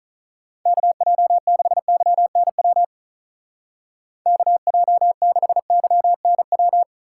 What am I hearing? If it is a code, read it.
Morse code, heard as a keyed tone, about 44 wpm, 700 Hz. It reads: KJ6YNW KJ6YNW
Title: Speed Racing - USA Call Signs 35wpm